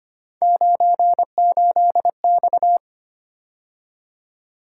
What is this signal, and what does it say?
Morse code, heard as a keyed tone, about 25 wpm, 700 Hz. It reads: 98X
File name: Three Letter-Number Combinations - 25wpm